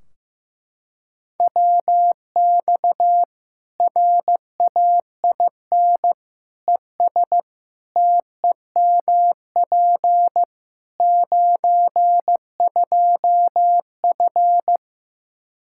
Morse code, heard as a keyed tone, about 15 wpm, 700 Hz. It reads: WX RAIN ES TEMP 92F
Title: Practice QSO - Exchange 2 - Info - 15wpm